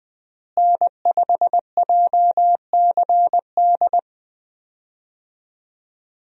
Morse code, heard as a keyed tone, about 20 wpm, 700 Hz. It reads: N5JCD
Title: USA Call Signs - 5 Characters - Repeated 2x - 20wpm